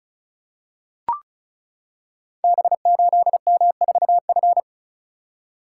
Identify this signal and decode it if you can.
Morse code, heard as a keyed tone, about 36 wpm, 690 Hz. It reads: B8M4F
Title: Code Groups - 5 Letter:Numbers - 35wpm